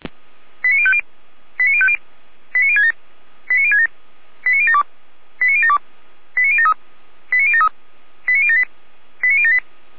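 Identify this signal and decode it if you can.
Five-tone selective call (ZVEI), heard as quick runs of five tones, tones of about 70 ms. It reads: 89050 89050 89067 89067 89072 89072 89073 89073 89078 89078
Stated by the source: Received on an SDR.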